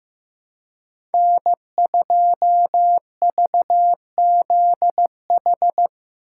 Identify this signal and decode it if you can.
Morse code, heard as a keyed tone, about 15 wpm, 700 Hz. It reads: N2VZH
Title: USA Call Signs - 5 Characters - 3x Character Spacing - 15wpm